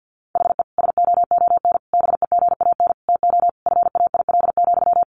Morse code, heard as a keyed tone, about 50 wpm, 700 Hz. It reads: HE SOON BEGAN TO FAIL,
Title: Sentences from Top 1100 Words 50wpm